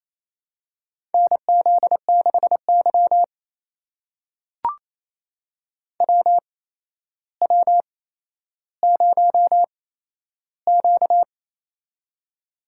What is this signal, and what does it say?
Morse code, heard as a keyed tone, about 27 wpm, spaced as for 12 wpm, 700 Hz. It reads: NZ6Y WW0Q
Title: USA Call Signs - 4 Characters - 8x Character Spacing - 28wpm